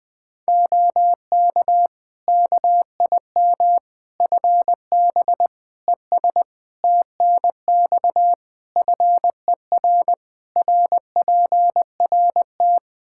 Morse code, heard as a keyed tone, about 20 wpm, 700 Hz. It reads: OK KIM FB ES TNX FER RPRT
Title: Practice QSO - Exchange 2 - Info - 20wpm